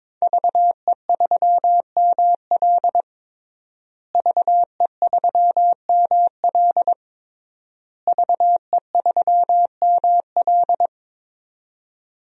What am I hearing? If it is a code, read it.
Morse code, heard as a keyed tone, about 22 wpm, 700 Hz. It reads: VE3ML VE3ML VE3ML
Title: DXCC-100 Call Signs - 5 Characters - Repeated 3x - 22wpm